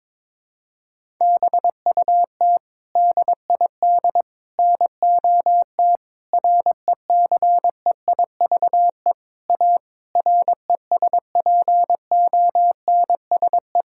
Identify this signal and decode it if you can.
Morse code, heard as a keyed tone, about 22 wpm, 700 Hz. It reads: BUT DID NOT RECEIVE A RESPONSE
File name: practice / 22wpm / Sets of 8 Words - 22wpm